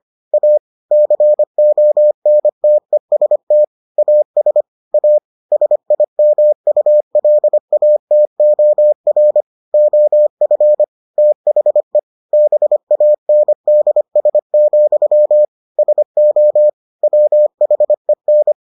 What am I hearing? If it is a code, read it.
Morse code, heard as a keyed tone, about 25 wpm, 600 Hz. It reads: A CONTEST AS A SIMULATOR OF THE BANDS, SO WHEN